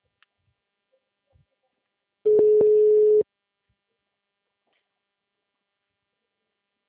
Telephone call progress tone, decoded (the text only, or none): ringback tone